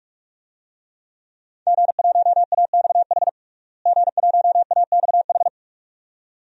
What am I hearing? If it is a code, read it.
Morse code, heard as a keyed tone, about 48 wpm, 700 Hz. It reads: G1AXH G1AXH
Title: DXCC-100 Call Signs - 5 Characters - Repeated 2x - 45wpm